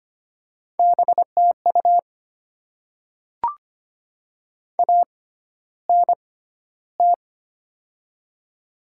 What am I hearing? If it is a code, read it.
Morse code, heard as a keyed tone, about 25 wpm, spaced as for 13 wpm, 700 Hz. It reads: BTU ANT
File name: Top 20 QSO Elements - 6x Character Spacing - 25wpm